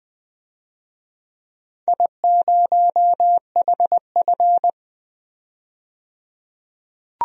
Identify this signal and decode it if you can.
Morse code, heard as a keyed tone, about 21 wpm, 700 Hz. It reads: I0HF
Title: DXCC-100 Call Signs - 4 Characters - 20wpm